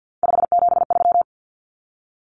Morse code, heard as a keyed tone, about 50 wpm, 700 Hz. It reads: 573
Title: Numbers - 3-Digit 50wpm